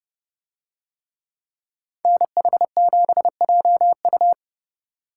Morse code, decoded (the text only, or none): NH7JU